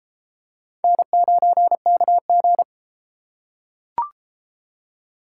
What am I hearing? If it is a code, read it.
Morse code, heard as a keyed tone, about 33 wpm, 700 Hz. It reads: N9KG